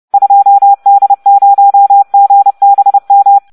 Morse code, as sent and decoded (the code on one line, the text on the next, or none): ..--- -.. ----- --. -... --
2D0GBM